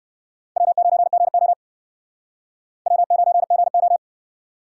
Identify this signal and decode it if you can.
Morse code, heard as a keyed tone, about 61 wpm, 700 Hz. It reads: W9GO W9GO